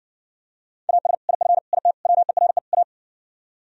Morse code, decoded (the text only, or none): AS IF IT WERE A